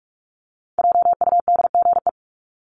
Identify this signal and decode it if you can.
Morse code, heard as a keyed tone, about 44 wpm, 700 Hz. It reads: JUDGE
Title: Top 1751-2000 Words 45wpm